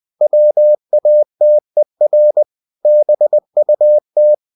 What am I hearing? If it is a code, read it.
Morse code, heard as a keyed tone, about 20 wpm, 600 Hz. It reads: WATER BUT
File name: the old man and the sea / the old man and the sea part 1 20wpm